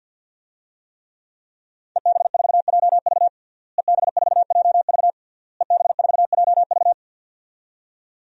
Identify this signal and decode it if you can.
Morse code, heard as a keyed tone, about 53 wpm, 700 Hz. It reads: EB4JV EB4JV EB4JV
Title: DXCC-100 Call Signs - 5 Characters - Repeated 3x - 50wpm